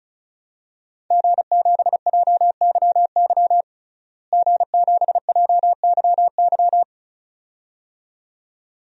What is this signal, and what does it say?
Morse code, heard as a keyed tone, about 34 wpm, 700 Hz. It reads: G7JYY G7JYY
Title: DXCC-100 Call Signs - 5 Characters - Repeated 2x - 35wpm